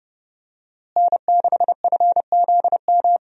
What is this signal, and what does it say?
Morse code, heard as a keyed tone, about 30 wpm, 700 Hz. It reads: N6FZM
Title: USA Call Signs - 5 Characters - Repeated 3x - 30wpm